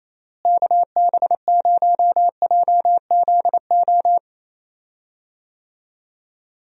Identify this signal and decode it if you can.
Morse code, heard as a keyed tone, about 28 wpm, 700 Hz. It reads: KB0JZO
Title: USA Call Signs - 6 Characters - Repeated 2x - 28wpm